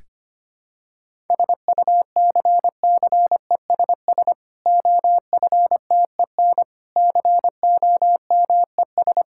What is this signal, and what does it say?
Morse code, heard as a keyed tone, about 25 wpm, 700 Hz. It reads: SUCCESS OFTEN COMES
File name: practice / 25wpm / Difficult Sentences with Hints - 25wpm